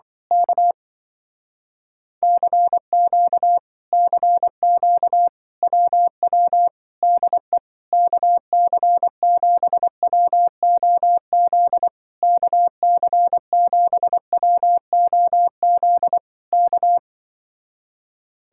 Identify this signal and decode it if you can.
Morse code, heard as a keyed tone, about 24 wpm, 700 Hz. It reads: K CQ CQ WW DE KC7WOZ KC7WOZ K
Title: Speed-Racing Common QSO Phrases Encore - 20wpm